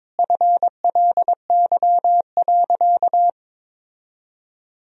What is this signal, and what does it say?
Morse code, heard as a keyed tone, about 22 wpm, 700 Hz. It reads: FLY.
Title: Sentences from Top 1000 Words - Review Part 2 - 22wpm